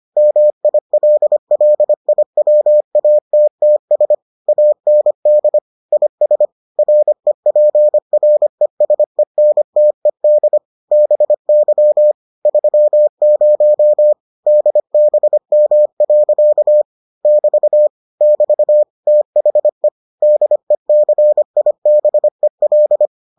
Morse code, heard as a keyed tone, about 25 wpm, 600 Hz. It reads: MILLIWATTS AND IS REPRESENTED BY 30 DBM. <BT> <BT> THE DECIBEL